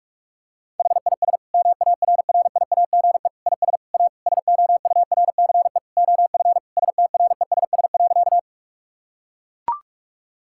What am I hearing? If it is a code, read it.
Morse code, heard as a keyed tone, about 41 wpm, 700 Hz. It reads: HIS MARRIAGE IS A SOURCE OF STRESS.
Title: Sentences from Top 1000 Words - Review Part 6 - 45wpm